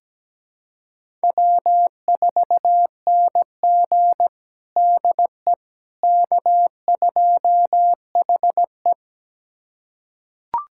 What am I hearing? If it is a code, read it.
Morse code, heard as a keyed tone, about 17 wpm, 700 Hz. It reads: W4NG DE K2HE
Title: USA Callsign DE Callsign - 4 Characters - 17wpm